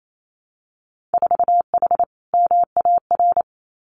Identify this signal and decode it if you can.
Morse code, heard as a keyed tone, about 28 wpm, 700 Hz. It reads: 4H MAR
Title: Winter Field Day Exchanges Encore 28wpm